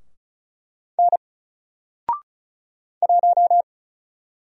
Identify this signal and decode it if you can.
Morse code, heard as a keyed tone, about 35 wpm, 700 Hz. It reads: N 1
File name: Single-Letter-Number-35wpm